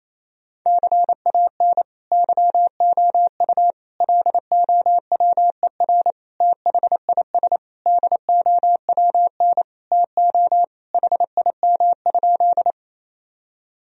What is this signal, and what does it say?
Morse code, heard as a keyed tone, about 28 wpm, 700 Hz. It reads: CAN YOU LOWER THIS DOWN TO HIM?
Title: Sentences from Top 1000 Words - Review Part 4 - 28wpm